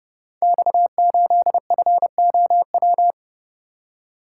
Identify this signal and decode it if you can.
Morse code, heard as a keyed tone, about 30 wpm, 700 Hz. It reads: X8FOW
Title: Code Groups - 5 Letter:Numbers - 30wpm